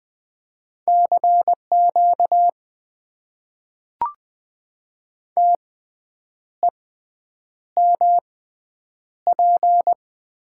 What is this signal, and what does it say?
Morse code, heard as a keyed tone, about 20 wpm, spaced as for 8 wpm, 700 Hz. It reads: CQ TEMP
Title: Top 90 QSO Elements - 6x Character Spacing - 20wpm